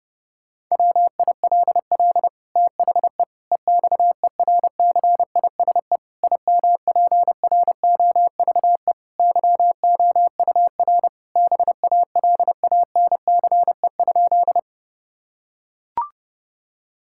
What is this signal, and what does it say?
Morse code, heard as a keyed tone, about 32 wpm, 700 Hz. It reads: WILL THE EXERCISE IMPROVE YOUR BALANCE?
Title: Sentences from Top 1000 Words - Review Part 6 - 30wpm